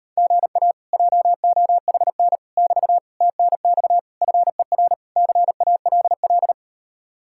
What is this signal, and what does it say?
Morse code, heard as a keyed tone, about 39 wpm, 700 Hz. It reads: GA JOHN <BT> TNX FER CALL